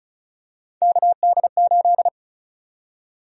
Morse code, heard as a keyed tone, about 36 wpm, 700 Hz. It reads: KD8
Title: USA Call Sign Prefixes - Repeated 2x - 35wpm